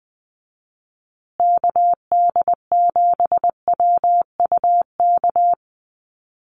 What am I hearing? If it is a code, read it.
Morse code, heard as a keyed tone, about 20 wpm, 700 Hz. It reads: KD7WUK
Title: USA Call Signs Repeated 3x 20wpm